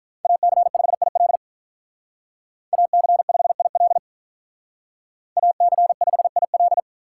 Morse code, heard as a keyed tone, about 45 wpm, 700 Hz. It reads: AC5IL AC5IL AC5IL